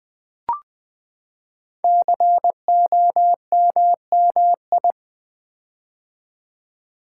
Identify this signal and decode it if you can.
Morse code, heard as a keyed tone, about 20 wpm, 700 Hz. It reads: COMMI
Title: Build A Word - 5 Letters At A Time - 10-Letter Words - 20wpm